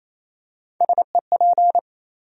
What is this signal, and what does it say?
Morse code, heard as a keyed tone, about 28 wpm, 700 Hz. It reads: SEP
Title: USA Call Sign Postfixes - 3 Letters 28wpm